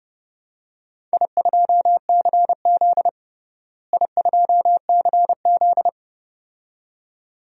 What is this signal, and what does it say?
Morse code, heard as a keyed tone, about 30 wpm, 700 Hz. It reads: I2CZ I2CZ
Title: DXCC-100 Call Signs - 4 Characters - Repeated 2x - 30wpm